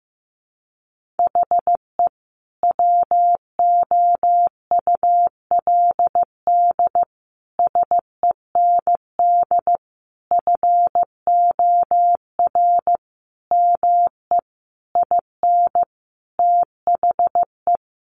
Morse code, heard as a keyed tone, about 15 wpm, 700 Hz. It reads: HE WOULD SEND FOR ME IN THE
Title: Sentences from Top 1500 Words 15wpm